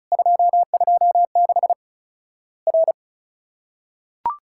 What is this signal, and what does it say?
Morse code, heard as a keyed tone, about 34 wpm, 700 Hz. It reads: 226 R
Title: Worked All Europe DX Contest - QTC Exchange - 35wpm